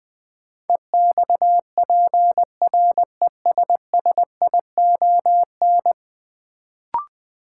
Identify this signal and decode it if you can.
Morse code, heard as a keyed tone, about 20 wpm, 700 Hz. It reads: EXPRESSION
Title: Sentences from Top 1200 Words - New Word Drill Alt - 20wpm